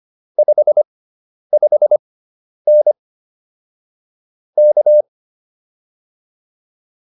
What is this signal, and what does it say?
Morse code, heard as a keyed tone, about 25 wpm, spaced as for 10 wpm, 600 Hz. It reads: 55N K